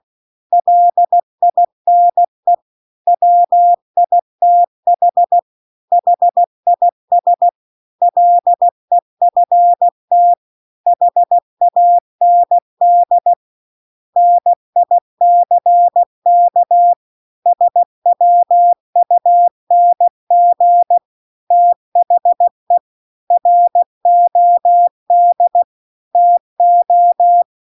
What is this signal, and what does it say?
Morse code, heard as a keyed tone, about 16 wpm, 700 Hz. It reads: LINE WITH HIS LEFT HAND NICK SWUNG THE ROD TO